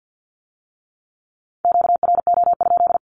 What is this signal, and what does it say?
Morse code, heard as a keyed tone, about 50 wpm, 700 Hz. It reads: QRO?
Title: Q-Codes with Spoken Meaning - 50wpm